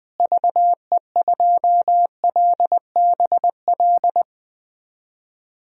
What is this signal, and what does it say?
Morse code, heard as a keyed tone, about 21 wpm, 700 Hz. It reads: VE2LBL
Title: DXCC-100 Call Signs - 6 Characters - Repeated 3x - 20wpm